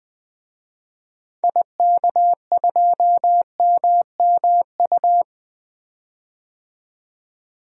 Morse code, heard as a keyed tone, about 20 wpm, 700 Hz. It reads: IK2MMU
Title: DXCC-100 Call Signs - 6 Characters - 20wpm